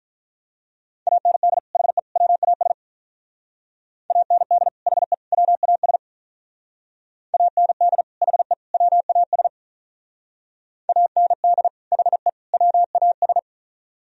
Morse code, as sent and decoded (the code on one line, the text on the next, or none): .- -. -.. / .... / .-- .- ... / .- -. -.. / .... / .-- .- ... / .- -. -.. / .... / .-- .- ... / .- -. -.. / .... / .-- .- ...
AND HE WAS AND HE WAS AND HE WAS AND HE WAS